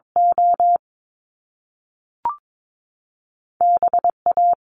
Morse code, heard as a keyed tone, about 22 wpm, 700 Hz. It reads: O B A
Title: Top Two Letter Combinations in Words 22wpm